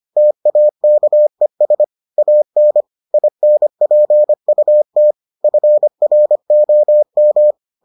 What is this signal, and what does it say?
Morse code, heard as a keyed tone, about 25 wpm, 600 Hz. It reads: TAKES AN INPUT FROM